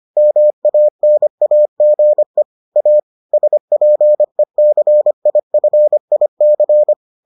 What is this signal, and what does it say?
Morse code, heard as a keyed tone, about 25 wpm, 600 Hz. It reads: MANAGE A SPECIFIC